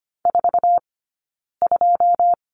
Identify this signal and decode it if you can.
Morse code, heard as a keyed tone, about 25 wpm, 700 Hz. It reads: 4 2
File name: Lesson 15d - Number 5 - Two Characters 25wpm